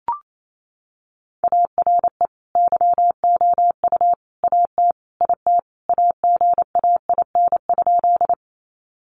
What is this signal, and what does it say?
Morse code, heard as a keyed tone, about 28 wpm, 700 Hz. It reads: ARE YOU AT IT AGAIN?
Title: Sentences from Top 200 Words 28wpm